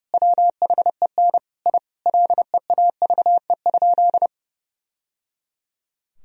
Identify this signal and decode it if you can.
Morse code, heard as a keyed tone, about 30 wpm, 700 Hz. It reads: WHEN I LEAVE?
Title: Sentences from Top 1000 Words - Review Part 2 - 30wpm